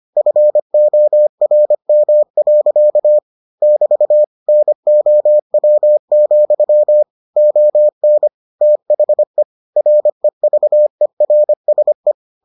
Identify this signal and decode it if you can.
Morse code, heard as a keyed tone, about 25 wpm, 600 Hz. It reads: FORM. <BT> NOW, ON THE REVERSE